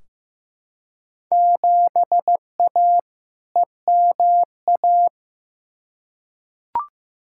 Morse code, heard as a keyed tone, about 15 wpm, 700 Hz. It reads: 7A EMA
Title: Field Day Exchanges Encore - 15wpm